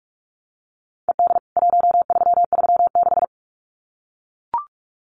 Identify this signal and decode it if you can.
Morse code, heard as a keyed tone, about 44 wpm, 700 Hz. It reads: ED 1336
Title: CWT Exchange 45wpm